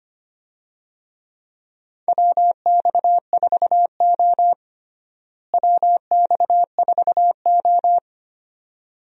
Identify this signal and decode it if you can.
Morse code, heard as a keyed tone, about 25 wpm, 700 Hz. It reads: WX4O WX4O
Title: USA Call Signs - 4 Characters - Repeated 2x - 25wpm